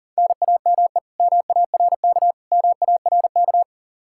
Morse code, heard as a keyed tone, about 40 wpm, 700 Hz. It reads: NAME MARK MARK
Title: Practice QSO - Exchange 1 - Beginning - 40wpm